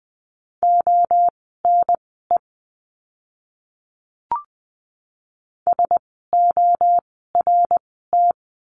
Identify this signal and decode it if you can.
Morse code, heard as a keyed tone, about 20 wpm, spaced as for 15 wpm, 700 Hz. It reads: ONE SORT
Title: 054 Lesson 10f - Letter r - Words 20wpm